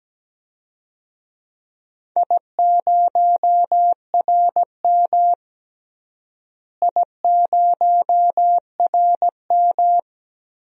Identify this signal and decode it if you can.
Morse code, heard as a keyed tone, about 17 wpm, 700 Hz. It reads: I0RM I0RM
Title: DXCC-100 Call Signs - 4 Characters - Repeated 2x - 17wpm